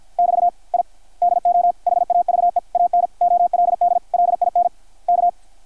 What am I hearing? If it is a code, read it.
Morse code, heard as a keyed tone, about 52 wpm, spaced as for 44 wpm, 660 Hz. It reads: <BT> I DO HAVE AN OLD LINK
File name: archived / CRSnet